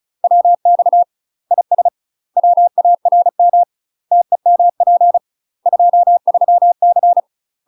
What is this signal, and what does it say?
Morse code, heard as a keyed tone, about 34 wpm, 700 Hz. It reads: WX IS WARM TEMP 23C
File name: UK MASTER QSOs TRAINING / 35WPM / UK MASTER QSOs 0005